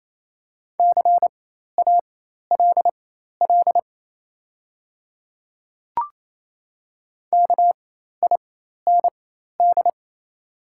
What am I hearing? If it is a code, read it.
Morse code, heard as a keyed tone, about 27 wpm, spaced as for 13 wpm, 700 Hz. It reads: CALL KIND
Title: 4x Character Spacing - Top 500 Words - 28wpm